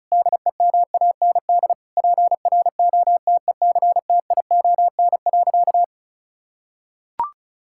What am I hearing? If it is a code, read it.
Morse code, heard as a keyed tone, about 36 wpm, 700 Hz. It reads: DEMAND PROTECTION.
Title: Sentences from Top 1600 Words 35wpm